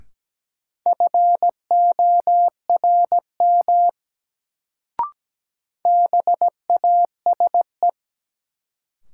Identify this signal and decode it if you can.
Morse code, heard as a keyed tone, about 17 wpm, 700 Hz. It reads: FORM BASE
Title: Sentences from Top 1000 Words - Review Part 1 - New Word Drill - 17wpm